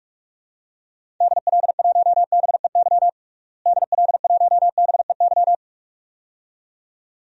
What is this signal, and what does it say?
Morse code, heard as a keyed tone, about 46 wpm, 700 Hz. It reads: DL1BEY DL1BEY
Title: DXCC-100 Call Signs - 6 Characters - Repeated 2x - 45wpm